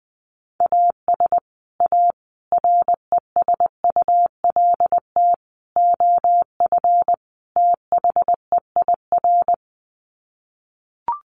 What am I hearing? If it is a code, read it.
Morse code, heard as a keyed tone, about 20 wpm, 700 Hz. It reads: AS A RESULT OF THEIR
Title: Sets of 5 Words Encore2 20wpm